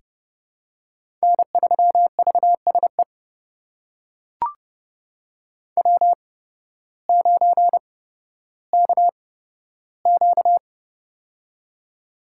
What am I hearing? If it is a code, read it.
Morse code, heard as a keyed tone, about 31 wpm, spaced as for 14 wpm, 700 Hz. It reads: N3VSE W9KQ
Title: USA Call Signs - 8x Character Spacing - 30wpm